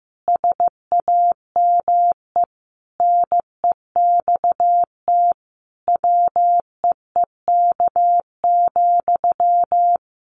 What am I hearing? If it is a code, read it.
Morse code, heard as a keyed tone, about 15 wpm, 700 Hz. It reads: SAME NEXT WEEK,
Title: Sentences from Top 500 Words 15wpm